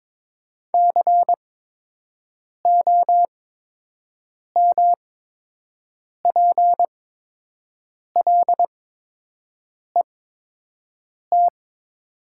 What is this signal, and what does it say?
Morse code, heard as a keyed tone, about 22 wpm, spaced as for 6 wpm, 700 Hz. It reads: COMPLET
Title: 8x Character Spacing - Top 500 Words - 22wpm